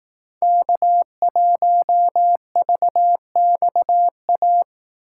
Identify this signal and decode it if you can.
Morse code, heard as a keyed tone, about 18 wpm, 700 Hz. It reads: K1VXA
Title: Speed Racing - USA 1x3 Call Signs 15wpm